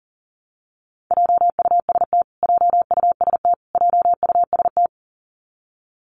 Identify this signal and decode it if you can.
Morse code, heard as a keyed tone, about 40 wpm, 700 Hz. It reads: JUST JUST JUST
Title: Top 500 Words Repeated 40wpm